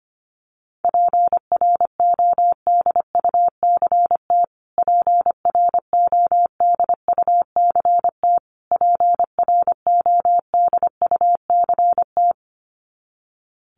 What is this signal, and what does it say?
Morse code, heard as a keyed tone, about 25 wpm, 700 Hz. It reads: PRODUCT PRODUCT PRODUCT
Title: Top 500 Words Repeated 25wpm